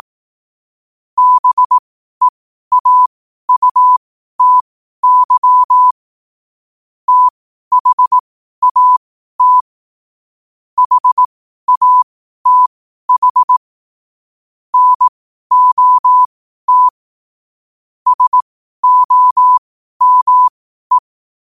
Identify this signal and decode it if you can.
Morse code, heard as a keyed tone, about 18 wpm, spaced as for 12 wpm, 1000 Hz. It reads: BEAUTY THAT HATH NOT SOME
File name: qotdfiles / 12wpm